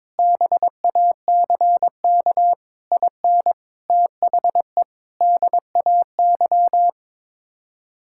Morse code, heard as a keyed tone, about 21 wpm, 700 Hz. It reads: BACK IN THE DAY
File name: practice / 22wpm / Sentences with 3-Word Phrases - 22wpm